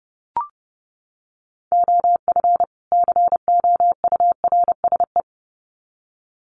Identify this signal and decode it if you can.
Morse code, heard as a keyed tone, about 30 wpm, 700 Hz. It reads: OF COURSE
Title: Sets of 2 Words 30wpm